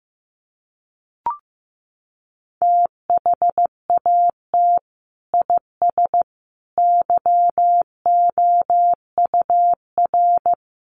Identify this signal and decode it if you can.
Morse code, heard as a keyed tone, about 15 wpm, 700 Hz. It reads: THAT IS YOUR
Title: Sentences from Top 1200 Words 15wpm